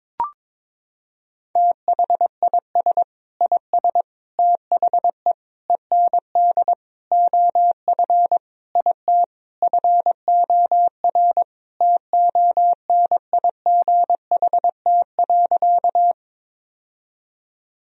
Morse code, heard as a keyed tone, about 22 wpm, 700 Hz. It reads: THIS IS THE END OF IT FOR TONIGHT.
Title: Sentences from Top 1000 Words - Review Part 3 - 22wpm